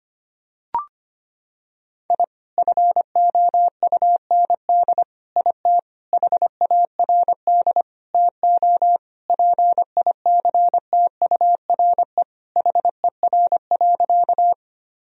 Morse code, heard as a keyed tone, about 25 wpm, 700 Hz. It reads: I FOUND IT HARD TO PICTURE HER.
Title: Sentences from Top 1000 Words - Review Part 1 - 25wpm